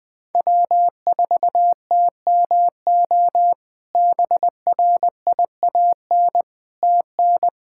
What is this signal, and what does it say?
Morse code, heard as a keyed tone, about 20 wpm, 700 Hz. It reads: W4TMO BRIAN TN
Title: K1USN SST QSO Practice - 20wpm